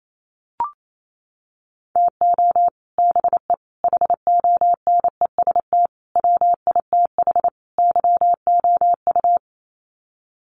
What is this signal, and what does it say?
Morse code, heard as a keyed tone, about 28 wpm, 690 Hz. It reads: TO BE HONEST WITH YOU
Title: Sets of 5 Words Encore 28wpm